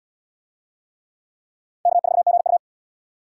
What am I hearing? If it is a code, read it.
Morse code, heard as a keyed tone, about 68 wpm, 700 Hz. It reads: K4GU